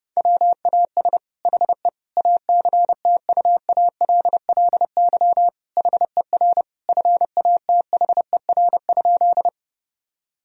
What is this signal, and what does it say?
Morse code, heard as a keyed tone, about 32 wpm, 700 Hz. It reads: WAS HE ACTUALLY HER FATHER?
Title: Sentences from Top 1000 Words - Review Part 4 - 30wpm